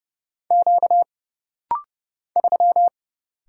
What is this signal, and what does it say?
Morse code, heard as a keyed tone, about 30 wpm, 700 Hz. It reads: Q 3
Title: Single Letter-Number - Rapid-Fire 30wpm